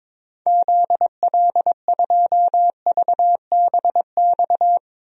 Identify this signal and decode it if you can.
Morse code, heard as a keyed tone, about 22 wpm, 700 Hz. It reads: ZL2VBX
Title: DXCC-100 Call Signs - 6 Characters - Repeated 2x - 22wpm